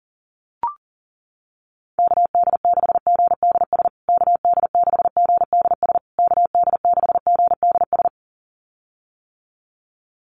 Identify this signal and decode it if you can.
Morse code, heard as a keyed tone, about 40 wpm, 700 Hz. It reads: KD6GDS KD6GDS KD6GDS
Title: USA Call Signs Repeated 3x 40wpm